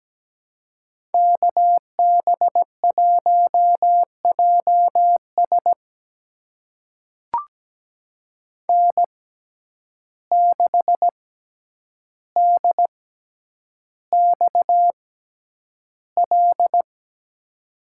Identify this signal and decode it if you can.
Morse code, heard as a keyed tone, about 17 wpm, spaced as for 9 wpm, 700 Hz. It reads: KB1JS N6DXL
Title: USA Call Signs - 5 Characters - 6x Character Spacing - 17wpm